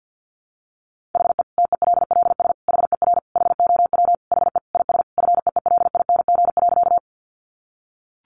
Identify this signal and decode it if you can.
Morse code, heard as a keyed tone, about 50 wpm, 700 Hz. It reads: HE TELLS HER HOW HE IS FEELING.
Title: Additional Easy Phrases 50wpm